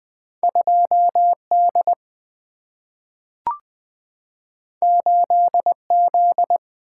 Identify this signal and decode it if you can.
Morse code, heard as a keyed tone, about 20 wpm, 700 Hz. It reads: 2D 8Z
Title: Single-Letter Single-Number 20wpm